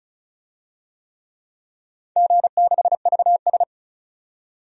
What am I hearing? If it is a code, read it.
Morse code, heard as a keyed tone, about 34 wpm, 700 Hz. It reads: G6VS
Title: DXCC-100 Call Signs - 4 Characters - 35wpm